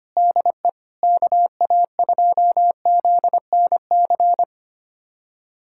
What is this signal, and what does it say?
Morse code, heard as a keyed tone, about 25 wpm, 700 Hz. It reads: DE KA2ZNC
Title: USA Callsign DE Callsign - 6 Characters - 25wpm